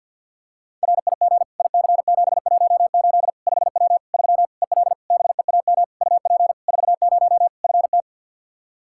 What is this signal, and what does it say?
Morse code, heard as a keyed tone, about 51 wpm, 700 Hz. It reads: RIG IC718 5W 3 EL BEAM UP 40 FT
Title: K1USN SST QSO Practice 50wpm